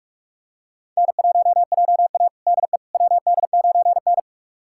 Morse code, heard as a keyed tone, about 46 wpm, 700 Hz. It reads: N1JA DE WD9N
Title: USA Callsign DE Callsign - 4 Characters - 45wpm